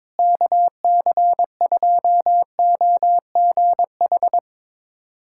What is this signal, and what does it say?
Morse code, heard as a keyed tone, about 23 wpm, 700 Hz. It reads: KC2OGH
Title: USA Callsign DE Callsign 22wpm